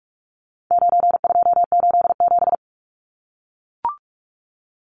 Morse code, heard as a keyed tone, about 44 wpm, 700 Hz. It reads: 9287
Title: Numbers - 4-Digit 45wpm